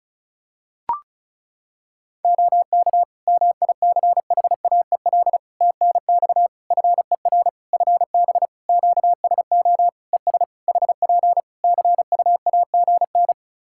Morse code, heard as a keyed tone, about 36 wpm, 700 Hz. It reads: OK MICHAEL TNX FER FB QSO ES HP CUAGN
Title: Practice QSO - Exchange 4 - Ending - 35wpm